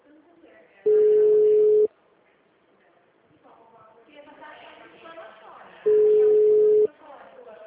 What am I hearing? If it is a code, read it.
Ringback tone (Europe, 425 Hz), with speech.
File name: out- -2014- - - .53680